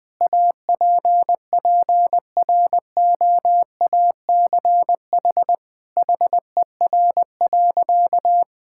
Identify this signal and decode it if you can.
Morse code, heard as a keyed tone, about 21 wpm, 700 Hz. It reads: APPROACH HER.
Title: Sentences from Top 1000 Words - Review Part 6 - 20wpm